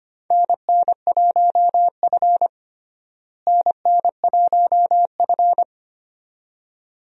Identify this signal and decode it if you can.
Morse code, heard as a keyed tone, about 25 wpm, 700 Hz. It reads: NN1F NN1F
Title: USA Call Signs - 4 Characters - Repeated 2x - 25wpm